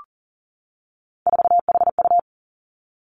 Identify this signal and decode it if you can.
Morse code, heard as a keyed tone, about 41 wpm, 700 Hz. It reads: 4HU
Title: Three Challenging Letter-Number Combinations 40wpm